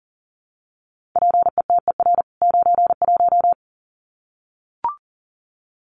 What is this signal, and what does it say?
Morse code, heard as a keyed tone, about 40 wpm, 700 Hz. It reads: PETER 91
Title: CWT Exchange 40wpm